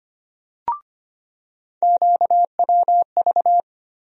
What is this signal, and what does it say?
Morse code, heard as a keyed tone, about 25 wpm, 700 Hz. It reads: QWV
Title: USA Call Sign Postfixes - 3 Letters 25wpm